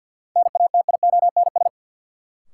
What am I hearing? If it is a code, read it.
Morse code, heard as a keyed tone, about 53 wpm, 700 Hz. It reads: NATIONS
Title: Sets of 3 Words - New Word Drill - 50wpm